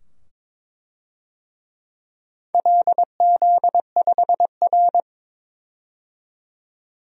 Morse code, heard as a keyed tone, about 23 wpm, 700 Hz. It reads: LZ5R
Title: DXCC-100 Call Signs - 4 Characters - 22wpm